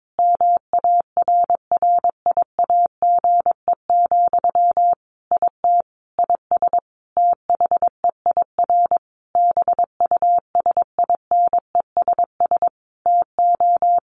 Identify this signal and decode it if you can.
Morse code, heard as a keyed tone, about 22 wpm, 700 Hz. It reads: MARRIAGE, IT IS THEIR BUSINESS TO
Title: Sentences from Top 1500 Words 22wpm